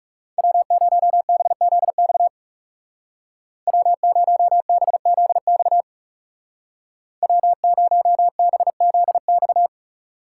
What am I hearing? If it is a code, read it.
Morse code, heard as a keyed tone, about 36 wpm, 700 Hz. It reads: W0BZX W0BZX W0BZX
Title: Speed Racing - USA 1x3 Call Signs 30wpm